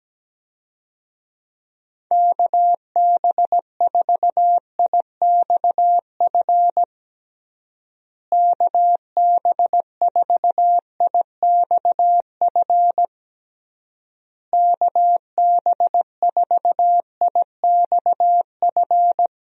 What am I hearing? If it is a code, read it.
Morse code, heard as a keyed tone, about 17 wpm, 700 Hz. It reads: KB4IXF KB4IXF KB4IXF
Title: USA Call Signs - 6 Characters - Repeated 3x - 17wpm